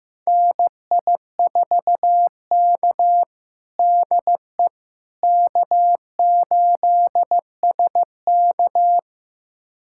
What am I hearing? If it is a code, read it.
Morse code, heard as a keyed tone, about 16 wpm, 700 Hz. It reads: NI4K DE K8SK
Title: USA Callsign DE Callsign - 4 Characters - 15wpm